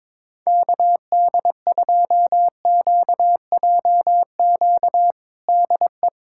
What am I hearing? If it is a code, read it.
Morse code, heard as a keyed tone, about 22 wpm, 700 Hz. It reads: KD2QJQ DE
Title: USA Callsign DE Callsign 22wpm